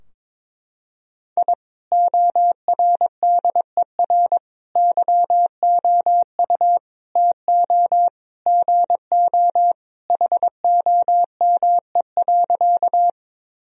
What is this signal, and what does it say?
Morse code, heard as a keyed tone, about 22 wpm, 700 Hz. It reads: I ORDER YOU TO GO HOME.
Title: Sentences from Top 1000 Words - Review Part 2 - 22wpm